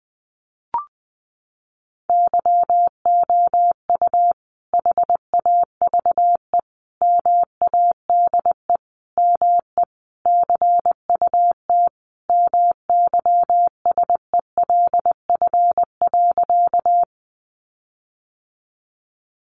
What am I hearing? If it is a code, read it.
Morse code, heard as a keyed tone, about 20 wpm, 700 Hz. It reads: YOU HAVE MADE ME CUT MYSELF.
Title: Sentences from Top 1300 Words 20wpm